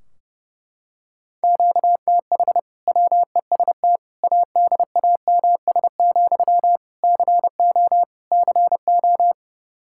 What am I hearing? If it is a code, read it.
Morse code, heard as a keyed tone, about 30 wpm, 700 Hz. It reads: QTH WEST ADAMS, CO CO
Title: Practice QSO - Exchange 1 - Beginning - 30wpm